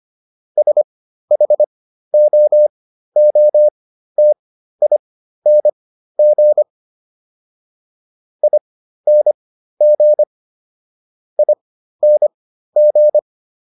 Morse code, heard as a keyed tone, about 25 wpm, spaced as for 13 wpm, 600 Hz. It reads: SHOOTING ING ING